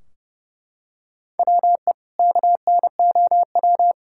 Morse code, heard as a keyed tone, about 30 wpm, 700 Hz. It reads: WE KNOW
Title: Sentences from Top 1600 Words 30wpm